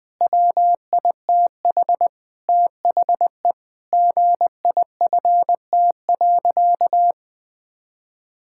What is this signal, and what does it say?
Morse code, heard as a keyed tone, about 20 wpm, 700 Hz. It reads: WITH THE GIFT.
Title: Sentences with 5-Word Phrases - Spoken Beginning - 20wpm